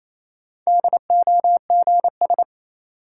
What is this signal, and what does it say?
Morse code, heard as a keyed tone, about 28 wpm, 700 Hz. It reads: DOGS
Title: Sentences from Top 1600 Words - New Word Drill - 28wpm